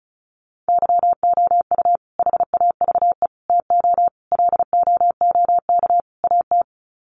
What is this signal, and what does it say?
Morse code, heard as a keyed tone, about 34 wpm, 700 Hz. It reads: YOU HAVE TO LOOK AT
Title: Sets of 5 Words Encore 35wpm